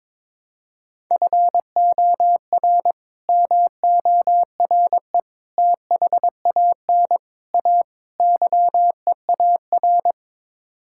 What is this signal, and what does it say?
Morse code, heard as a keyed tone, about 22 wpm, 700 Hz. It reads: FOR MORE THAN A YEAR
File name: practice / 22wpm / Build A Phrase - 5-Word Phrases - 22wpm